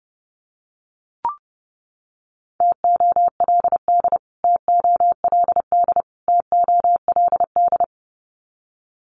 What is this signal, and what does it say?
Morse code, heard as a keyed tone, about 30 wpm, 700 Hz. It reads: TOLD TOLD TOLD
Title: Top 500 Words Repeated 30wpm